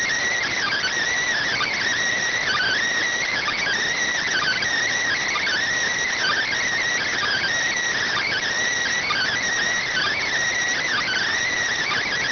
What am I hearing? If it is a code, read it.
Emergency Alert System (SAME) tones (header bursts), heard as harsh buzzing bursts. It is heard from 0.3 to 11.3 s.